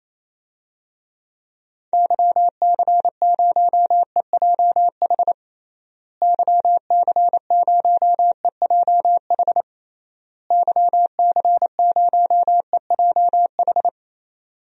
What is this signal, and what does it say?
Morse code, heard as a keyed tone, about 28 wpm, 700 Hz. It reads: YC0EJH YC0EJH YC0EJH
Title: DXCC-100 Call Signs - 6 Characters - Repeated 3x - 28wpm